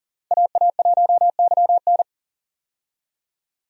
Morse code, heard as a keyed tone, about 40 wpm, 700 Hz. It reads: AA1YN
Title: USA Call Signs - 5 Characters - 40wpm